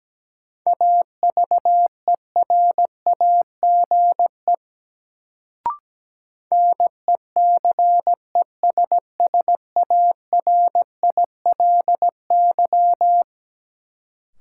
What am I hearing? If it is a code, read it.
Morse code, heard as a keyed tone, about 17 wpm, 700 Hz. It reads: AVERAGE NECESSARILY
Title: Sentences from Top 1400 Words - New Word Drill Alt - 17wpm